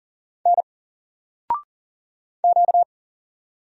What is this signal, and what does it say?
Morse code, heard as a keyed tone, about 40 wpm, 700 Hz. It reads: N Q
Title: Single Letter-Number Alt 40wpm